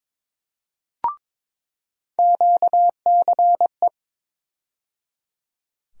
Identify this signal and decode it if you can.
Morse code, heard as a keyed tone, about 22 wpm, 700 Hz. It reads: QCE
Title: USA Call Sign Postfixes - 3 Letters 22wpm